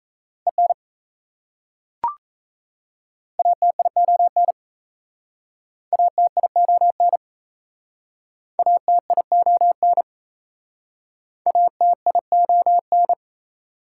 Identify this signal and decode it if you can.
Morse code, heard as a keyed tone, about 36 wpm, 700 Hz. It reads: EN ATION ATION ATION ATION